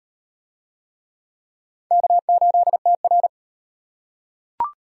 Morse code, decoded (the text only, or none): K8TR